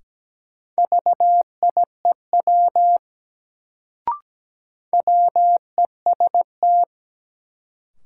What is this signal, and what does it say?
Morse code, heard as a keyed tone, about 17 wpm, 700 Hz. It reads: VIEW WEST
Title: Sets of 5 Words - New Word Drill - 17wpm